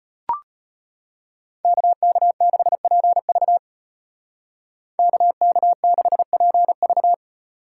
Morse code, heard as a keyed tone, about 32 wpm, 700 Hz. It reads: KK6PV KK6PV